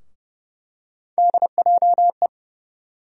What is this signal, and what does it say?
Morse code, heard as a keyed tone, about 30 wpm, 700 Hz. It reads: DJE
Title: USA Call Sign Postfixes - 3 Letters 30wpm